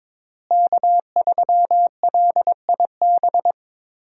Morse code, heard as a keyed tone, about 21 wpm, 700 Hz. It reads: K3LIB
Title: USA Call Signs - 5 Characters - Repeated 3x - 22wpm